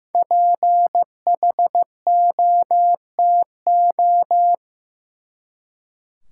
Morse code, heard as a keyed tone, about 15 wpm, 700 Hz. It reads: PHOTO
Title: Five Letter Words - 15wpm